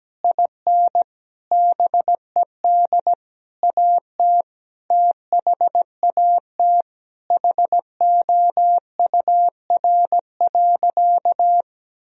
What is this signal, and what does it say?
Morse code, heard as a keyed tone, about 17 wpm, 700 Hz. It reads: IN BED AT THAT HOUR.
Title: Sentences from Top 1000 Words - Review Part 3 - 17wpm